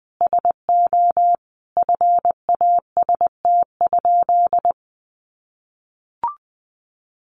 Morse code, heard as a keyed tone, about 19 wpm, 700 Hz. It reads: SO FAST?
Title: Sentences from Top 500 Words 20wpm